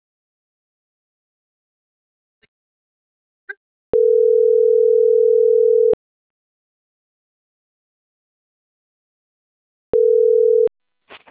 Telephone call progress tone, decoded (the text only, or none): ringback tone